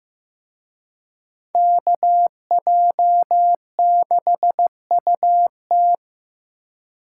Morse code, heard as a keyed tone, about 18 wpm, 700 Hz. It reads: KJ6UT
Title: Speed Racing - USA 2x2 Call Signs 15wpm